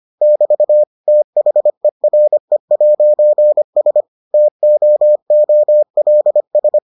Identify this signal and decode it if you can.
Morse code, heard as a keyed tone, about 25 wpm, 600 Hz. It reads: <BT> THERE'S TOOLS